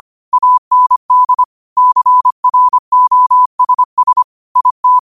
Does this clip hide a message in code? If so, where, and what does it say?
Yes: Morse code from 0.3 to 5.0 s, heard as a keyed tone, about 25 wpm, 1000 Hz. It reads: AND CROSS IT